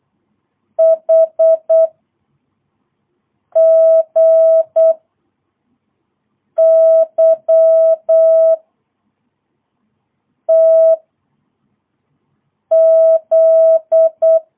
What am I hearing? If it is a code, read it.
Morse code, heard as a keyed tone, about 8 wpm, 660 Hz. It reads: H G Y T Z
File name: jh8cbh / 2022cw